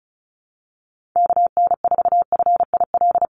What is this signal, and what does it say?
Morse code, heard as a keyed tone, about 34 wpm, 700 Hz. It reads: KN4FIL
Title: USA Call Signs Repeated 3x 35wpm